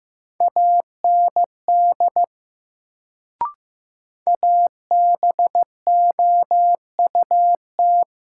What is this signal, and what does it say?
Morse code, heard as a keyed tone, about 15 wpm, 700 Hz. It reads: AND ABOUT
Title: Sentences from Top 100 Words - New Word Drill - 15wpm